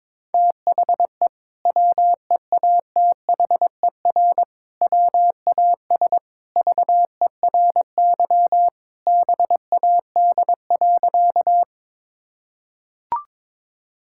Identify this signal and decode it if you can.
Morse code, heard as a keyed tone, about 22 wpm, 700 Hz. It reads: THE WEATHER WAS VERY BAD.
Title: Sentences from Top 1000 Words - Review Part 5 - 22wpm